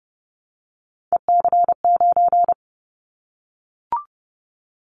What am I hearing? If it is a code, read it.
Morse code, heard as a keyed tone, about 29 wpm, 700 Hz. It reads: EC9
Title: Callsign Prefixes for Top 100 DXCC - 30wpm